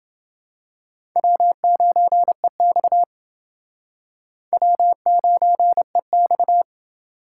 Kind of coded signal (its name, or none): Morse code